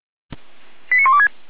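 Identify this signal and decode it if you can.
Five-tone selective call (ZVEI), heard as quick runs of five tones, tones of about 70 ms. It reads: 89126
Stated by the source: ZVEI five-tone call received on SDR, 455 channel.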